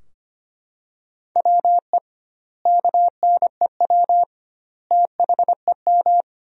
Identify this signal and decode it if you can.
Morse code, heard as a keyed tone, about 25 wpm, 700 Hz. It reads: WE KNEW THEM
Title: 2x Word-Spacing - Sentences from Top 400 Words - 25wpm